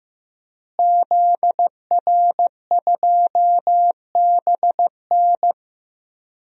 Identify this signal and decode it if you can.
Morse code, heard as a keyed tone, about 15 wpm, 700 Hz. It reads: ZR2BN
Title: DXCC-100 Call Signs - 5 Characters - Repeated 3x - 15wpm